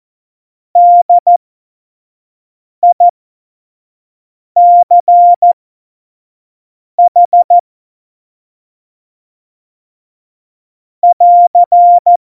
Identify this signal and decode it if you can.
Morse code, heard as a keyed tone, about 14 wpm, spaced as for 5 wpm, 700 Hz. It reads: DICH <AR>
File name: Morsekurs / Witze / Witz 24. Oktober 17, Tempo 25